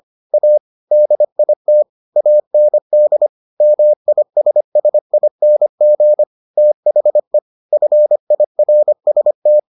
Morse code, heard as a keyed tone, about 25 wpm, 600 Hz. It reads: A DIT AND MISSING THE FIRST